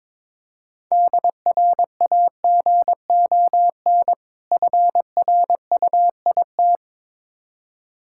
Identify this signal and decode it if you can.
Morse code, heard as a keyed tone, about 21 wpm, 690 Hz. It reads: DRAGON FRUIT
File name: Difficult Words with Hints - 22wpm